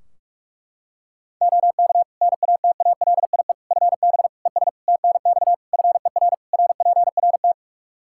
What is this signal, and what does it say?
Morse code, heard as a keyed tone, about 41 wpm, 700 Hz. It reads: OK NATALIE FB ES TNX FER RPRT
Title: Practice QSO - Exchange 2 - Info - 45wpm